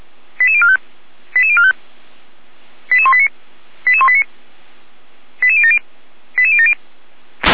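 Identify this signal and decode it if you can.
Five-tone selective call (ZVEI), heard as quick runs of five tones, tones of about 80 ms. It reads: 88045 88045 88189 88189 88070 88070
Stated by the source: ZVEI five-tone call 88090 received on SDR, Feuerwehr Orlamünde.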